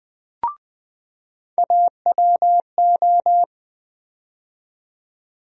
Morse code, heard as a keyed tone, about 20 wpm, 700 Hz. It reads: AWO
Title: USA Call Sign Postfixes - 3 Letters 20wpm